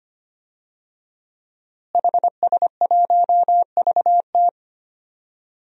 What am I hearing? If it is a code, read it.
Morse code, heard as a keyed tone, about 25 wpm, 700 Hz. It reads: HS1VT
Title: DXCC-100 Call Signs - 5 Characters - Repeated 3x - 25wpm